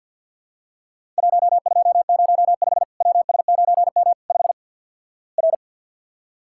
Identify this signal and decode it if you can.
Morse code, heard as a keyed tone, about 50 wpm, 700 Hz. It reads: 1205 WS9M 5 R
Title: Worked All Europe DX Contest - QTC Exchange with Repeats - 50wpm